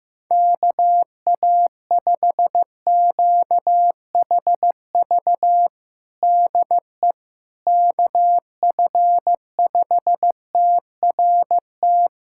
Morse code, heard as a keyed tone, about 15 wpm, 700 Hz. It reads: KA5QHV DE KF5TRT
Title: USA Callsign DE Callsign - 6 Characters - 15wpm